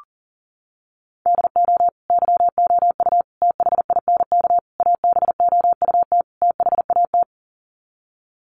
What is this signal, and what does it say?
Morse code, heard as a keyed tone, about 40 wpm, 700 Hz. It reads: DO YOU THINK ABOUT THAT
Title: Sets of 5 Words Encore2 40wpm